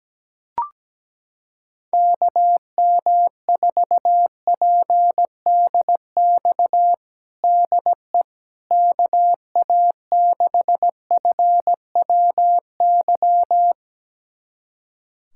Morse code, heard as a keyed tone, about 17 wpm, 700 Hz. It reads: KM4PDX DE KA6FWY
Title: USA Callsign DE Callsign - 6 Characters - 17wpm